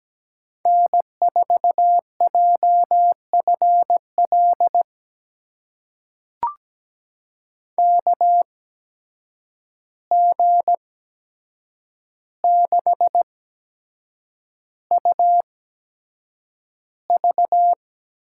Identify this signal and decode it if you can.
Morse code, heard as a keyed tone, about 17 wpm, spaced as for 7 wpm, 700 Hz. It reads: N4JFL KG6UV
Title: USA Call Signs - 5 Characters - 8x Character Spacing - 17wpm